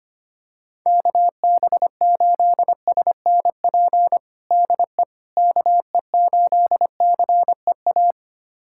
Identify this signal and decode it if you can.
Morse code, heard as a keyed tone, about 25 wpm, 700 Hz. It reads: KB8SNP DE KE8CEA
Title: USA Callsign DE Callsign - 6 Characters - 25wpm